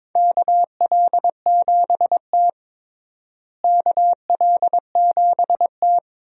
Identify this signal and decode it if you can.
Morse code, heard as a keyed tone, about 22 wpm, 700 Hz. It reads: KL7T KL7T
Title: USA Call Signs - 4 Characters - Repeated 2x - 22wpm